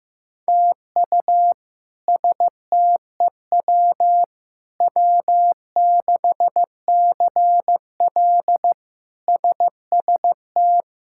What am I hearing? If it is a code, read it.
Morse code, heard as a keyed tone, about 15 wpm, 700 Hz. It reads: TU STEW W6CL SST
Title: K1USN SST QSO Practice - 15wpm